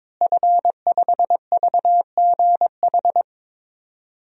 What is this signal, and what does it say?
Morse code, heard as a keyed tone, about 22 wpm, 700 Hz. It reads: F5VGH
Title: DXCC-100 Call Signs - 5 Characters - Repeated 3x - 22wpm